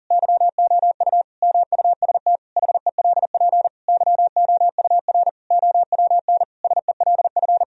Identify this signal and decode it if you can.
Morse code, heard as a keyed tone, about 40 wpm, 700 Hz. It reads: YOU MUST HELP YOUR OWN SELF
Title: Sentences from Top 1000 Words - Review Part 1 - 40wpm